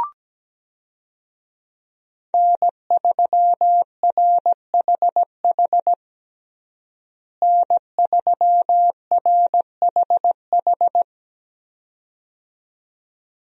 Morse code, heard as a keyed tone, about 17 wpm, 700 Hz. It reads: N3RHH N3RHH
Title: USA Call Signs - 5 Characters - Repeated 2x - 17wpm